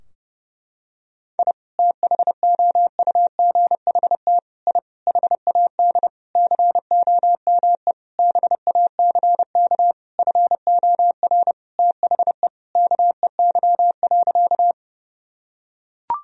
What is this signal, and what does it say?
Morse code, heard as a keyed tone, about 32 wpm, 700 Hz. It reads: I THOUGHT I HAD COME BACK FOR THE KEY.
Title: Sentences from Top 1000 Words - Review Part 4 - 30wpm